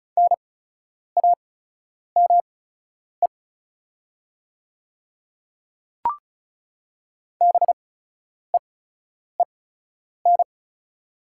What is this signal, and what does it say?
Morse code, heard as a keyed tone, about 36 wpm, spaced as for 10 wpm, 690 Hz. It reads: NAME BEEN
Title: 8x Character Spacing - Top 300 Words - 35wpm